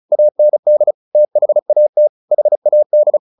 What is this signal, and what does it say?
Morse code, heard as a keyed tone, about 34 wpm, 600 Hz. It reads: AND THAT HAD
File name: the old man and the sea 2 35wpm